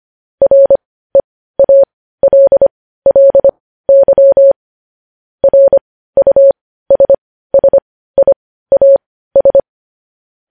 Morse code, heard as a keyed tone, about 25 wpm, spaced as for 15 wpm, 550 Hz. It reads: REALLY RUSSIAS
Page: news-headlines-in-morse-code-at-15-wpm